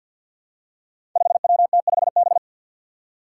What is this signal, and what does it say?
Morse code, heard as a keyed tone, about 50 wpm, 700 Hz. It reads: 5WT5B
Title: Five Challenging Letter-Number Combinations 50wpm